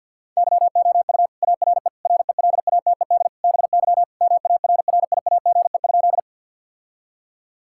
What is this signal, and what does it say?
Morse code, heard as a keyed tone, about 50 wpm, 700 Hz. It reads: YOU ARE RELATED BY MARRIAGE?
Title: Sentences from Top 1000 Words - Review Part 5 - 50wpm